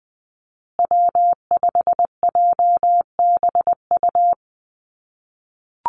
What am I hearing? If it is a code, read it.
Morse code, heard as a keyed tone, about 20 wpm, 700 Hz. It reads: W5JBU
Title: USA Call Signs Repeated 3x 20wpm